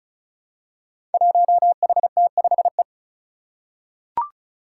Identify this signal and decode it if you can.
Morse code, heard as a keyed tone, about 36 wpm, 700 Hz. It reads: 1HT5E
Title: Five Challenging Letter-Number Combinations 35wpm